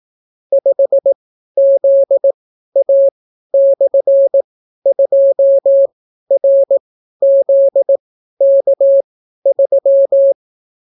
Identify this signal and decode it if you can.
Morse code, heard as a keyed tone, about 18 wpm, spaced as for 12 wpm, 550 Hz. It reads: 5ZA/2RZK3